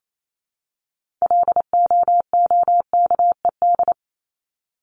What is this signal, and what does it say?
Morse code, heard as a keyed tone, about 28 wpm, 700 Hz. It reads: LOOKED
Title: ED Suffix - Top 100 Words with Base Words 28wpm